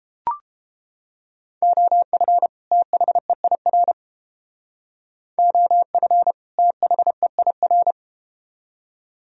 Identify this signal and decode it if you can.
Morse code, heard as a keyed tone, about 30 wpm, 700 Hz. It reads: OF THEIR OF THEIR